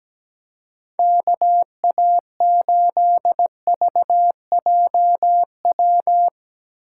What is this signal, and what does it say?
Morse code, heard as a keyed tone, about 18 wpm, 700 Hz. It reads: KA8VJW
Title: Speed Racing - USA Call Signs 17wpm